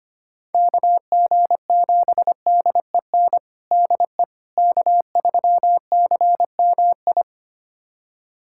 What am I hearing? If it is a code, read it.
Morse code, heard as a keyed tone, about 25 wpm, 700 Hz. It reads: KG7DEN DE K3CMI
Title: USA Callsign DE Callsign 25wpm